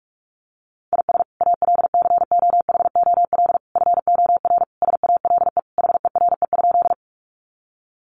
Morse code, heard as a keyed tone, about 44 wpm, 700 Hz. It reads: IS ALCOHOL FOR SALE HERE?
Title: Sentences from Top 1400 Words 45wpm